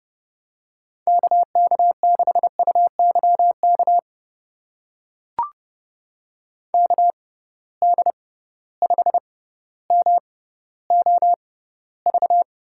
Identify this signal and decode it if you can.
Morse code, heard as a keyed tone, about 30 wpm, spaced as for 15 wpm, 700 Hz. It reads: KK6UYK KD5MOV
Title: USA Call Signs - 6 Characters - 6x Character Spacing - 30wpm